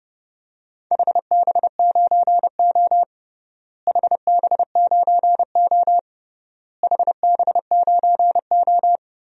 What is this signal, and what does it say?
Morse code, heard as a keyed tone, about 32 wpm, 700 Hz. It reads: HB9O HB9O HB9O
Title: DXCC-100 Call Signs - 4 Characters - Repeated 3x - 30wpm